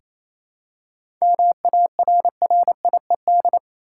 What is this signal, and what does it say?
Morse code, heard as a keyed tone, about 28 wpm, 700 Hz. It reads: MARRIED
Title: Seven Letter Words - 28wpm